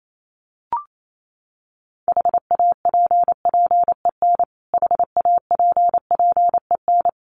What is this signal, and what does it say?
Morse code, heard as a keyed tone, about 28 wpm, 700 Hz. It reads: HAPPEN HAPPEN
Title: Top 500 Words Repeated 28wpm